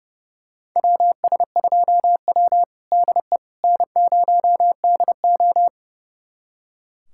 Morse code, heard as a keyed tone, about 30 wpm, 700 Hz. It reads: WS2W DE N0DO
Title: USA Callsign DE Callsign - 4 Characters - 30wpm